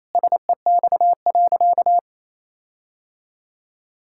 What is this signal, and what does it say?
Morse code, heard as a keyed tone, about 28 wpm, 700 Hz. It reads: SEX.
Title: Sentences from Top 1600 Words 28wpm